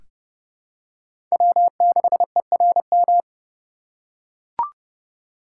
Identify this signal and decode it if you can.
Morse code, heard as a keyed tone, about 30 wpm, 700 Hz. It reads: W6ERM
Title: USA Call Signs - 5 Characters - 4x Character Spacing - 30wpm